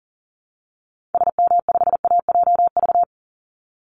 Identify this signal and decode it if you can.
Morse code, heard as a keyed tone, about 40 wpm, 700 Hz. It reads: SM5AJV
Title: Call Signs Repeated 3x - 40wpm